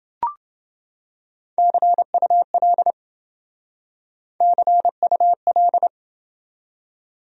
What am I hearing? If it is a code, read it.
Morse code, heard as a keyed tone, about 27 wpm, 700 Hz. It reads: CUL CUL